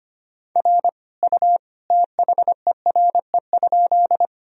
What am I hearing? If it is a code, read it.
Morse code, heard as a keyed tone, about 25 wpm, 700 Hz. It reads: R U THERE?
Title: Sentences from Top 100 Words and Abbreviations 25wpm